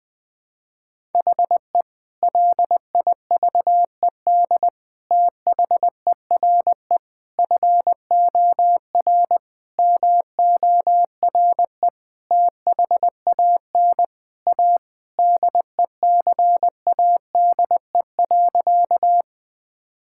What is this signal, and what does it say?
Morse code, heard as a keyed tone, about 20 wpm, 700 Hz. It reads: HE LIVED THERE FOR MORE THAN A DECADE.
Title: Sentences with 5-Word Phrases - 20wpm